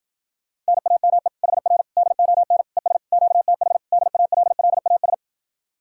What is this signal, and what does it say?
Morse code, heard as a keyed tone, about 52 wpm, 700 Hz. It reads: NAME HR DON ES QTH DALLAS